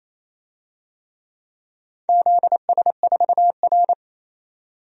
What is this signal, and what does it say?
Morse code, heard as a keyed tone, about 28 wpm, 700 Hz. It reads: ZS4R
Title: DXCC-100 Call Signs - 4 Characters - 28wpm